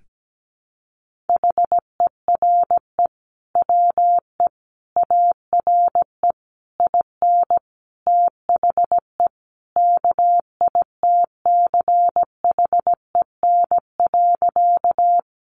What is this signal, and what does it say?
Morse code, heard as a keyed tone, about 17 wpm, 700 Hz. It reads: HERE WE ARE IN THE KITCHEN.
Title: Sentences from Top 1400 Words 17wpm